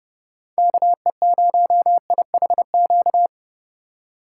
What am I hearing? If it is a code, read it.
Morse code, heard as a keyed tone, about 30 wpm, 700 Hz. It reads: KE0IHQ